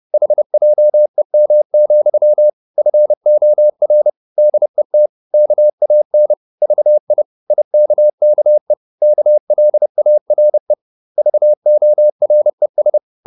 Morse code, heard as a keyed tone, about 30 wpm, 600 Hz. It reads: HJEM, FOR DET KAN VI IKKE KLARE VORES
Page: Ferie 30 wpm